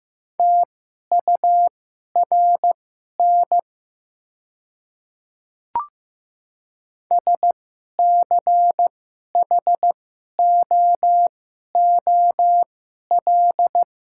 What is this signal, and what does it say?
Morse code, heard as a keyed tone, about 15 wpm, spaced as for 11 wpm, 700 Hz. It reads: TURN SCHOOL
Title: 2x Character Spacing - Top 300 Words - 15wpm